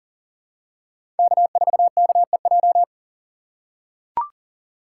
Morse code, heard as a keyed tone, about 37 wpm, 700 Hz. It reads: K4KEJ
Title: USA Call Signs - 6x Character Spacing - 40wpm